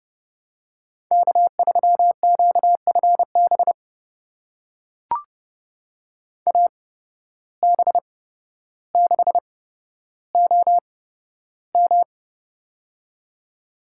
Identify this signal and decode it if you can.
Morse code, heard as a keyed tone, about 32 wpm, spaced as for 13 wpm, 700 Hz. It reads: K3QFB AB6OM
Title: USA Call Signs - 5 Characters - 8x Character Spacing - 30wpm